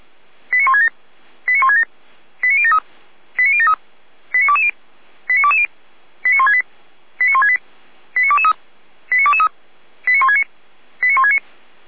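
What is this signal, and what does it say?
Five-tone selective call (ZVEI), heard as quick runs of five tones, tones of about 70 ms. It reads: 89167 89167 89073 89073 89220 89220 89168 89168 89223 89223 89169 89169
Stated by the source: ZVEI five-tone call 89022 received on an SDR, logged by feuerwehr-orlamuende.org fire brigade.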